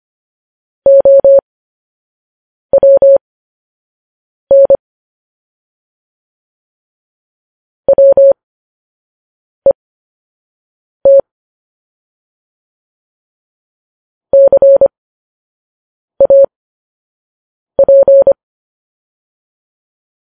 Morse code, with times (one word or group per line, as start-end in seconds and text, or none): 0.9-4.8 OWN
7.9-11.2 WET
14.3-18.3 CAP